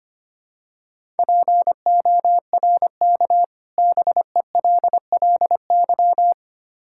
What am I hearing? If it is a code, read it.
Morse code, heard as a keyed tone, about 25 wpm, 700 Hz. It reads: PORK BELLY
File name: practice / 25wpm / Difficult Words with Hints - 25wpm